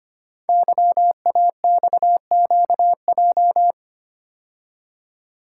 Morse code, heard as a keyed tone, about 25 wpm, 700 Hz. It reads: YAXQJ
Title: Code Groups - 5 Letters - 25wpm